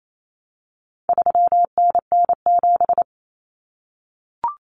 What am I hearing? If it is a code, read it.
Morse code, heard as a keyed tone, about 28 wpm, 700 Hz. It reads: 3NN7
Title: Common Cut Numbers - 4-Digit 28wpm